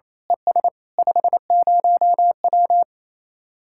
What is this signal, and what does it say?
Morse code, heard as a keyed tone, about 26 wpm, 700 Hz. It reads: ES 50W
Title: Speed-Racing Common QSO Phrases - 28wpm